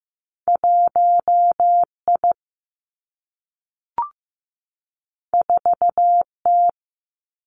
Morse code, heard as a keyed tone, about 15 wpm, 700 Hz. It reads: 1 I 4 T
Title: Two Challenging Letter-Number Combinations 15wpm